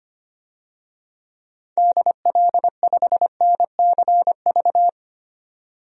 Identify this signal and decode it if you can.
Morse code, heard as a keyed tone, about 25 wpm, 700 Hz. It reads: DL5NCV
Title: DXCC-100 Call Signs - 6 Characters - 25wpm